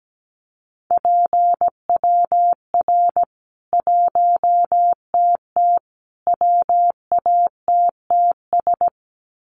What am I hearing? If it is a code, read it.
Morse code, heard as a keyed tone, about 17 wpm, 700 Hz. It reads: PWR 1TT WATTS
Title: Common QSO Phrases 17wpm